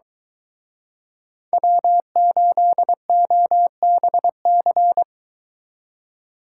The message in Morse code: W8OBC

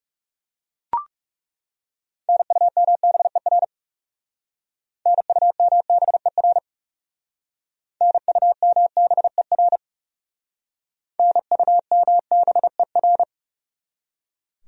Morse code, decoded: NUMBER NUMBER NUMBER NUMBER

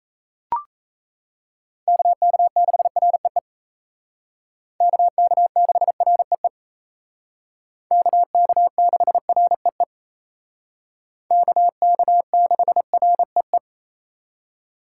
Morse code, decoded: KK6REE KK6REE KK6REE KK6REE

